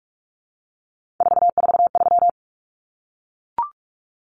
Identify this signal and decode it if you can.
Morse code, heard as a keyed tone, about 44 wpm, 700 Hz. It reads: 443
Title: Common Cut Numbers - 3-Digit 45wpm